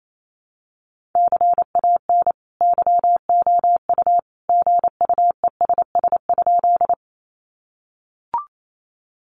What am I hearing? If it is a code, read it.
Morse code, heard as a keyed tone, about 28 wpm, 700 Hz. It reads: CAN YOU GUESS?
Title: Sentences from Top 1500 Words 28wpm